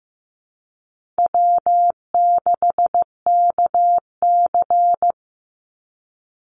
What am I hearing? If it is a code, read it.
Morse code, heard as a keyed tone, about 15 wpm, 700 Hz. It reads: W6KC
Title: Call Signs Repeated 3x - 15wpm